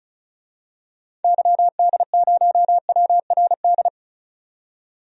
Morse code, heard as a keyed tone, about 36 wpm, 700 Hz. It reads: YD0WRD
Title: DXCC-100 Call Signs - 6 Characters - 35wpm